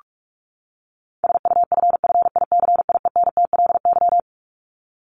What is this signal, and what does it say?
Morse code, heard as a keyed tone, about 44 wpm, 700 Hz. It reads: SUFFICIENTLY
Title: LY Suffix - Top 100 Words with Base Words 45wpm